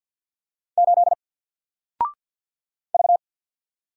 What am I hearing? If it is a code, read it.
Morse code, heard as a keyed tone, about 48 wpm, 700 Hz. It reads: 8 V
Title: Single Letter-Number 50wpm